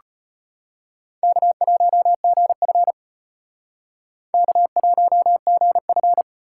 Morse code, heard as a keyed tone, about 32 wpm, 700 Hz. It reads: K1GF K1GF